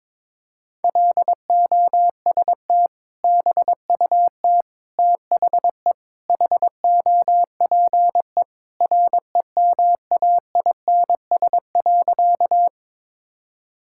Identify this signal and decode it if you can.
Morse code, heard as a keyed tone, about 22 wpm, 700 Hz. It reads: LOST BUT THE HOPE REMAINS.
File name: practice / 22wpm / Sentences with 2-Word Phrases - 22wpm